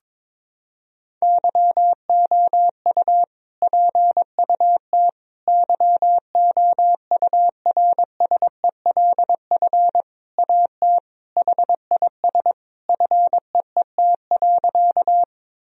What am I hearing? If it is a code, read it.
Morse code, heard as a keyed tone, about 22 wpm, 700 Hz. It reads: YOU PUT YOURSELF AT HIS FEET.
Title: Sentences from Top 1000 Words - Review Part 1 - 22wpm